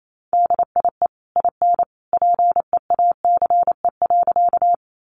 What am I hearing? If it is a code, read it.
Morse code, heard as a keyed tone, about 28 wpm, 700 Hz. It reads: DIE IN PEACE.
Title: Sentences from Top 1500 Words 28wpm